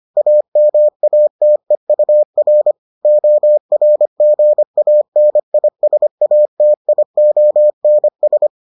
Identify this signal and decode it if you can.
Morse code, heard as a keyed tone, about 25 wpm, 600 Hz. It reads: AMATEUR ORGANISATIONS